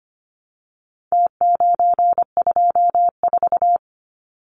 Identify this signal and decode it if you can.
Morse code, heard as a keyed tone, about 25 wpm, 700 Hz. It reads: T924
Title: Numbers - Contest Style - 4-Digit 25wpm